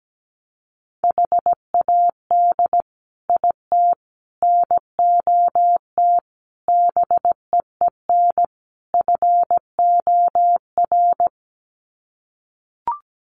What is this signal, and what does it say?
Morse code, heard as a keyed tone, about 17 wpm, 700 Hz. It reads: HAD IT NOT BEEN FOR
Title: Sets of 5 Words Encore2 17wpm